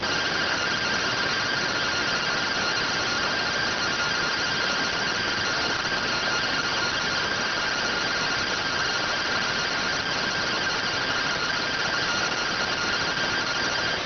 Two-tone teleprinter signal, about 170 Hz shift.